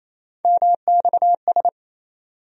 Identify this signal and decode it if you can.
Morse code, heard as a keyed tone, about 28 wpm, 700 Hz. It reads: MXS
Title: USA Call Sign Postfixes - 3 Letters 28wpm